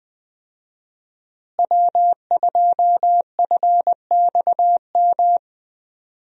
Morse code, heard as a keyed tone, about 20 wpm, 700 Hz. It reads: W2FXM
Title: USA Call Signs - 5 Characters - Repeated 2x - 20wpm